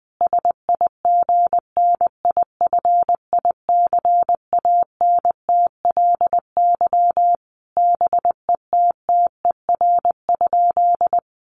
Morse code, heard as a keyed tone, about 19 wpm, 700 Hz. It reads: SIGNIFICANTLY BETTER?
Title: Sentences from Top 1500 Words 20wpm